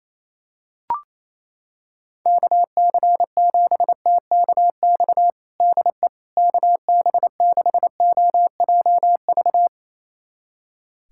Morse code, heard as a keyed tone, about 28 wpm, 700 Hz. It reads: KC7TKX DE KB6OJV
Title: USA Callsign DE Callsign - 6 Characters - 28wpm